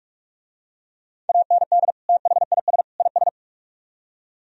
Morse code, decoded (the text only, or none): AND THIS IS